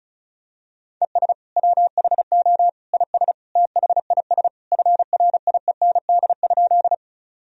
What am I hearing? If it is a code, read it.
Morse code, heard as a keyed tone, about 35 wpm, 700 Hz. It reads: ES WHO IS THIS FRIEND?
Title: Sentences from Top 300 Words and Abbreviations 35wpm